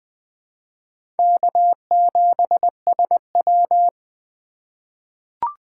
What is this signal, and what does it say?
Morse code, heard as a keyed tone, about 21 wpm, 700 Hz. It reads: K7SW
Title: USA Call Signs - 4 Characters - 3x Character Spacing - 20wpm